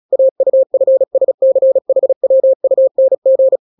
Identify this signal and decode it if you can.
Morse code, heard as a keyed tone, about 36 wpm, 530 Hz. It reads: AUFSCHWUNG
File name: daily / 35wpm